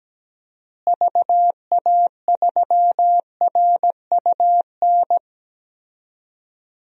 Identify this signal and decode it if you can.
Morse code, heard as a keyed tone, about 17 wpm, 700 Hz. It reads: VA3RUN
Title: DXCC-100 Call Signs - 6 Characters - Repeated 3x - 17wpm